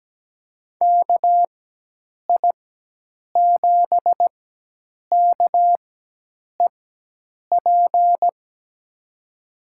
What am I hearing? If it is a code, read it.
Morse code, heard as a keyed tone, about 17 wpm, spaced as for 8 wpm, 700 Hz. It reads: KI7KEP
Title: USA Call Signs - 4x Character Spacing - 17wpm